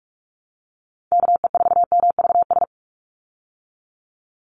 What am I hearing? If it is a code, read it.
Morse code, heard as a keyed tone, about 44 wpm, 700 Hz. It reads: KE4MVS
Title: USA Call Signs 45wpm